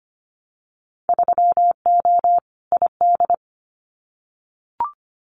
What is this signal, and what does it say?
Morse code, heard as a keyed tone, about 25 wpm, 690 Hz. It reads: 3O ID
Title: Winter Field Day Exchanges Encore 25wpm